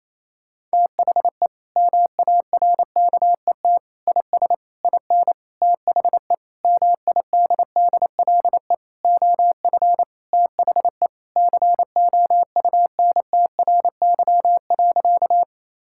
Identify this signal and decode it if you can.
Morse code, heard as a keyed tone, about 28 wpm, 700 Hz. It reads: THE MARKET IS IN THE MIDDLE OF THE COUNTRY.
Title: Sentences from Top 1000 Words - Review Part 6 - 28wpm